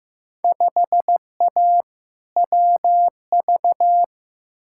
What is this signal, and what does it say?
Morse code, heard as a keyed tone, about 15 wpm, 700 Hz. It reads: 5A WV
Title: Field Day Exchanges - 15wpm